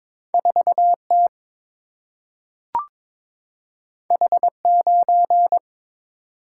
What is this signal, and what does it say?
Morse code, heard as a keyed tone, about 22 wpm, 700 Hz. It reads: 4 T H 9
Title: Single-Letter Single-Number 22wpm